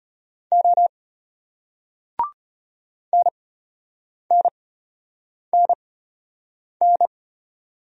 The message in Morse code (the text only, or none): O NNNN